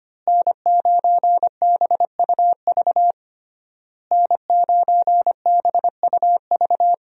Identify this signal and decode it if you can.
Morse code, heard as a keyed tone, about 25 wpm, 700 Hz. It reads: N9BUV N9BUV
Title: USA Call Signs - 5 Characters - Repeated 3x - 25wpm